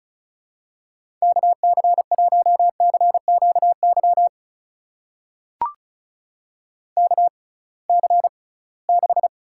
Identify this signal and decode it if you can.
Morse code, heard as a keyed tone, about 34 wpm, spaced as for 22 wpm, 700 Hz. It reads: KC1CQY KC6
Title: USA Call Signs - 6x Character Spacing - 35wpm